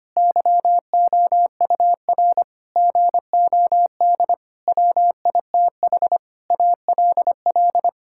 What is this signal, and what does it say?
Morse code, heard as a keyed tone, about 25 wpm, 700 Hz. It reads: YOUR GOD WITH ALL
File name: practice / 25wpm / Sentences with 9-Word Phrases - 25wpm